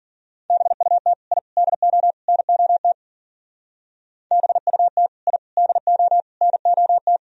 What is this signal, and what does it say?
Morse code, heard as a keyed tone, about 51 wpm, 700 Hz. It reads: BUT I DO NOT BUT I DO NOT